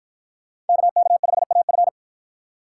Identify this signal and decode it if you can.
Morse code, heard as a keyed tone, about 49 wpm, 700 Hz. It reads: KK5AF